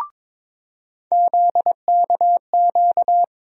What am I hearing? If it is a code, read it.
Morse code, heard as a keyed tone, about 22 wpm, 700 Hz. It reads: ZKQ
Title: USA Call Sign Postfixes - 3 Letters 22wpm